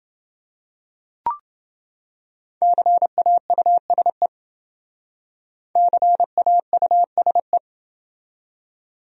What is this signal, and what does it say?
Morse code, heard as a keyed tone, about 28 wpm, 700 Hz. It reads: CAUSE CAUSE